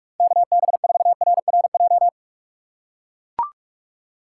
Morse code, heard as a keyed tone, about 44 wpm, 700 Hz. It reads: KB4RRJ